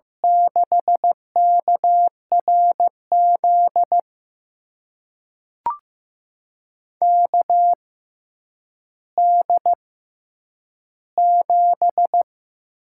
Morse code, heard as a keyed tone, about 15 wpm, spaced as for 9 wpm, 700 Hz. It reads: 6KRZ KD7
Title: USA Call Signs - 6 Characters - 6x Character Spacing - 15wpm